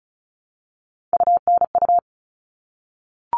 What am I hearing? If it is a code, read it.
Morse code, heard as a keyed tone, about 34 wpm, 700 Hz. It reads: UNU
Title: All Cut Numbers - 3-Digit 35wpm